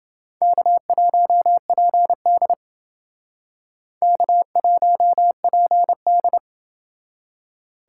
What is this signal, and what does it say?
Morse code, heard as a keyed tone, about 28 wpm, 700 Hz. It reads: K1PD K1PD